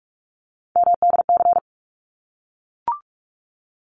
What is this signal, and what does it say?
Morse code, heard as a keyed tone, about 44 wpm, 700 Hz. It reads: MDC
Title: ARRL Sections 45wpm